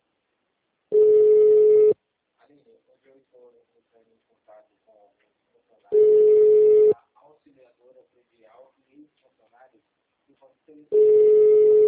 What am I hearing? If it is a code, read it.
Ringback tone (Europe, 425 Hz), with speech.